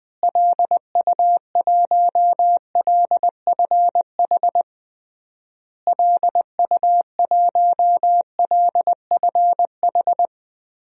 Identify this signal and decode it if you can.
Morse code, heard as a keyed tone, about 21 wpm, 700 Hz. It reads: LU1LFH LU1LFH
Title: DXCC-100 Call Signs - 6 Characters - Repeated 3x - 20wpm